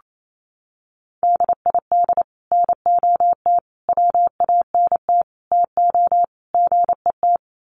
Morse code, heard as a keyed tone, about 28 wpm, 700 Hz. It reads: DID NOT WANT TO GET
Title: Sets of 5 Words Encore2 28wpm